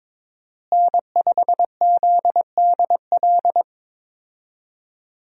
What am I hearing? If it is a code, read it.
Morse code, heard as a keyed tone, about 22 wpm, 700 Hz. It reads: N5ZDL
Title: USA Call Signs - 5 Characters - Repeated 3x - 22wpm